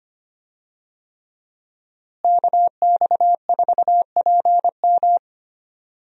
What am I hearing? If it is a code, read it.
Morse code, heard as a keyed tone, about 25 wpm, 700 Hz. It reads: KX4PM
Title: USA Call Signs - 5 Characters - Repeated 3x - 25wpm